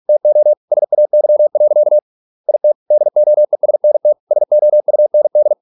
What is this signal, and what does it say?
Morse code, heard as a keyed tone, about 46 wpm, 600 Hz. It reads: TO SAY. IT DOESNT SOUND